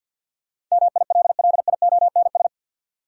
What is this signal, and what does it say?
Morse code, heard as a keyed tone, about 50 wpm, 700 Hz. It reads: MILLIONS
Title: Sets of 5 Words - New Word Drill - 50wpm